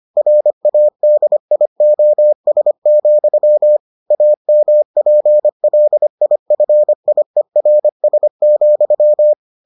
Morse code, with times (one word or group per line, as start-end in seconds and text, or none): 0.2-3.8 RADIOS,
4.1-9.3 AMPLIFIERS,